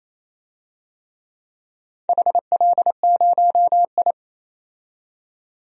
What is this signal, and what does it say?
Morse code, heard as a keyed tone, about 27 wpm, 690 Hz. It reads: HL0I
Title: DXCC-100 Call Signs - 4 Characters - 28wpm